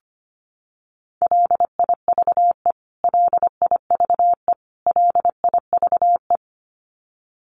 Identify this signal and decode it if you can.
Morse code, heard as a keyed tone, about 25 wpm, 700 Hz. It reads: LIVE LIVE LIVE
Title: Top 200 Words Repeated 25wpm